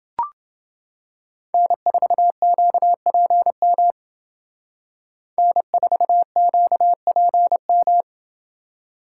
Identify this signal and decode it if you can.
Morse code, heard as a keyed tone, about 29 wpm, 700 Hz. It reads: N4QPM N4QPM